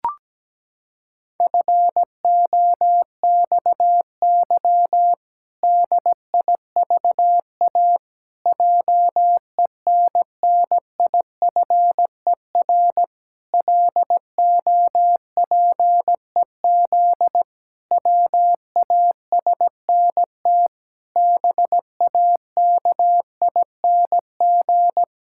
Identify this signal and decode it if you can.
Morse code, heard as a keyed tone, about 17 wpm, 700 Hz. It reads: FOXY DIVA JENNIFER LOPEZ WASNT BAKING
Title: Pangram Challenge v2 - 17wpm